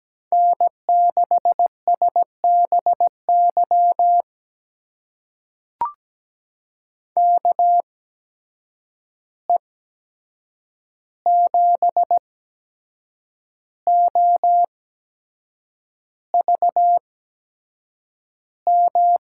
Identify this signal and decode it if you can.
Morse code, heard as a keyed tone, about 17 wpm, spaced as for 7 wpm, 700 Hz. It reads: N6SBY KE7OVM
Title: USA Call Signs - 8x Character Spacing - 17wpm